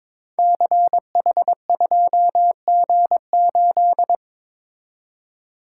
Morse code, heard as a keyed tone, about 22 wpm, 700 Hz. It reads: CH2G8
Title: Code Groups - 5 Letter:Numbers - 22wpm